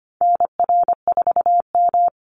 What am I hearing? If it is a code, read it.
Morse code, heard as a keyed tone, about 25 wpm, 700 Hz. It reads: NR4M
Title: Call Signs Repeated 2x - 25wpm